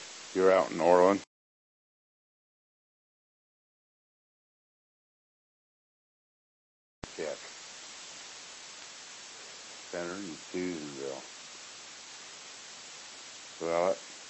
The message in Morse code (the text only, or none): none